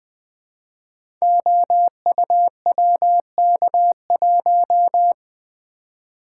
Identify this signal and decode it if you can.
Morse code, heard as a keyed tone, about 20 wpm, 700 Hz. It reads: OUWK1
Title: Code Groups - 5 Letter:Numbers - 20wpm